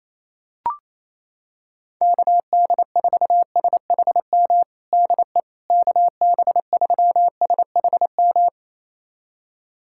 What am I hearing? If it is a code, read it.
Morse code, heard as a keyed tone, about 28 wpm, 700 Hz. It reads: KD4SHM DE KB3SHM
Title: USA Callsign DE Callsign - 6 Characters - 28wpm